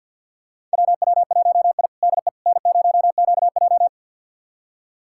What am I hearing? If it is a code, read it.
Morse code, heard as a keyed tone, about 52 wpm, 700 Hz. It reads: WW1I DE N0QJ
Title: USA Callsign DE Callsign - 4 Characters - 50wpm